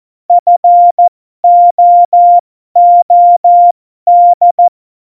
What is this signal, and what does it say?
Morse code, heard as a keyed tone, about 14 wpm, spaced as for 12 wpm, 700 Hz. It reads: FOOD
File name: Morsekurs / Witze